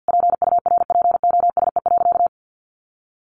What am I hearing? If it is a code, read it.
Morse code, heard as a keyed tone, about 50 wpm, 700 Hz. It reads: PURPOSE.
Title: Sentences from Top 800 Words 50wpm